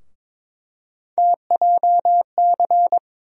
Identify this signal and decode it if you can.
Morse code, heard as a keyed tone, about 22 wpm, 700 Hz. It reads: TJC
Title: USA Call Sign Postfixes - 3 Letters 22wpm